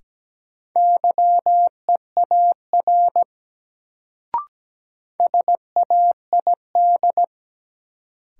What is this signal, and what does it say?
Morse code, heard as a keyed tone, about 17 wpm, 700 Hz. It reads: YEAR SAID
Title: Sets of 2 Words - New Word Drill - 17wpm